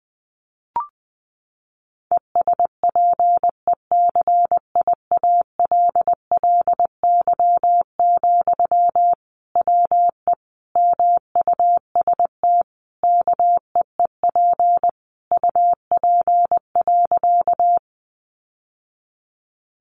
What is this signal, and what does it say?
Morse code, heard as a keyed tone, about 20 wpm, 700 Hz. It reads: ESPECIALLY, WE MUST KEEP UP.
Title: Sentences from Top 700 Words 20wpm